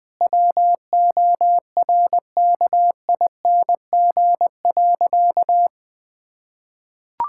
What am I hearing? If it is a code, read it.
Morse code, heard as a keyed tone, about 20 wpm, 700 Hz. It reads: WORKING.
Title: Sentences with 5-Word Phrases - Spoken Beginning - 20wpm